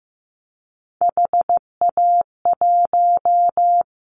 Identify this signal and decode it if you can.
Morse code, heard as a keyed tone, about 15 wpm, 700 Hz. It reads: HA1
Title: Callsign Prefixes for Top 50 DXCC - 15wpm